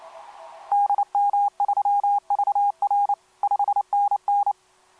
Morse code, heard as a keyed tone, about 28 wpm, 810 Hz. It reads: DM3VR 5NN
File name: kp2-ks9w sm1cq etc 12m cw 13mar2011 1339